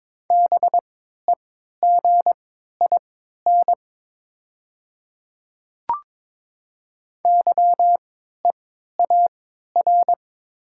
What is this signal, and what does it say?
Morse code, heard as a keyed tone, about 22 wpm, spaced as for 13 wpm, 700 Hz. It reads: BEGIN YEAR
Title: 3x Character Spacing - Top 300 Words - 22wpm